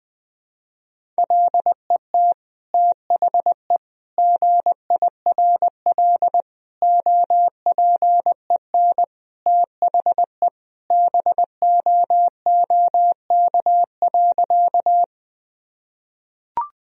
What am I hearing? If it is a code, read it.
Morse code, heard as a keyed tone, about 20 wpm, 700 Hz. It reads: LET THE GIRL OPEN THE BOOK.
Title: Sentences from Top 1000 Words - Review Part 1 - 20wpm